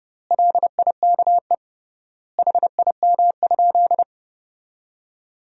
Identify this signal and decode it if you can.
Morse code, heard as a keyed tone, about 29 wpm, 700 Hz. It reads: LIKE HIM?
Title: 3x Word-Spacing - Sentences from Top 100 Words - 30wpm